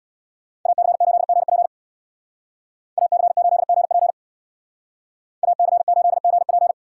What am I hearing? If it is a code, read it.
Morse code, heard as a keyed tone, about 64 wpm, 700 Hz. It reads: AC8GP AC8GP AC8GP